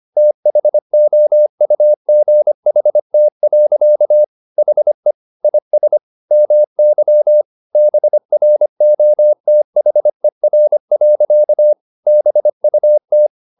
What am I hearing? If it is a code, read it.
Morse code, heard as a keyed tone, about 25 wpm, 600 Hz. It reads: THOUGHT. HE IS MY BROTHER. BUT
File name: the old man and the sea part 1 25wpm